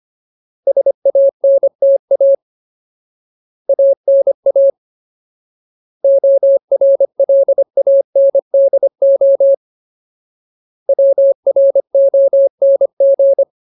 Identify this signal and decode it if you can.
Morse code, heard as a keyed tone, about 25 wpm, 550 Hz. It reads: SANTA ANA ORLANDO WRONG
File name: Ultimate Word List / 25WPM Ultimate